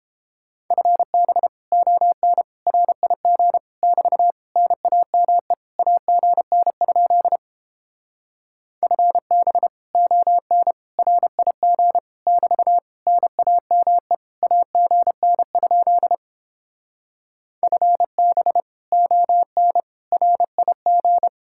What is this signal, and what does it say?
Morse code, heard as a keyed tone, about 31 wpm, 700 Hz. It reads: FB ON RIG <BT> NAME AGN? FB ON RIG <BT> NAME AGN? FB ON RIG